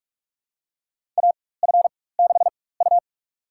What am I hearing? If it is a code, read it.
Morse code, heard as a keyed tone, about 46 wpm, spaced as for 21 wpm, 700 Hz. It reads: AF6U
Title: USA Call Signs - 4 Characters - 4x Character Spacing - 45wpm